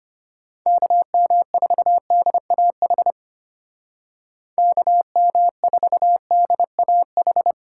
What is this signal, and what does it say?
Morse code, heard as a keyed tone, about 33 wpm, 700 Hz. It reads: KM4DAH KM4DAH